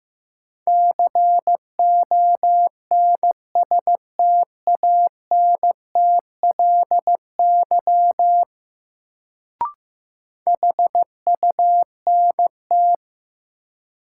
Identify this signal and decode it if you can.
Morse code, heard as a keyed tone, about 15 wpm, 700 Hz. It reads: CONSTANTLY HUNT
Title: Sentences from Top 1600 Words - New Word Drill Alt - 15wpm